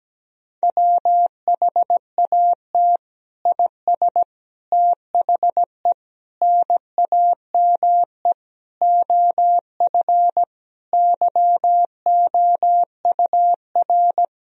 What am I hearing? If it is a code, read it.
Morse code, heard as a keyed tone, about 17 wpm, 700 Hz. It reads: WHAT IS THE NAME OF YOUR
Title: Sentences from Top 1000 Words - Review Part 2 - 17wpm